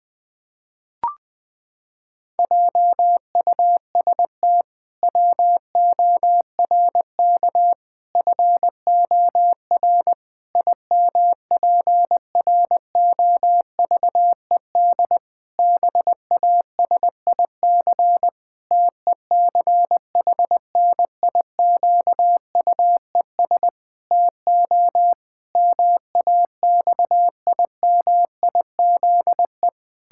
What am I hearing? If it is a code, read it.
Morse code, heard as a keyed tone, about 21 wpm, 700 Hz. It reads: JUST WORK FOR IMPROVED BASIC TECHNIQUES TO MAXIMIZE
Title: Pangram Challenge v2 - 20wpm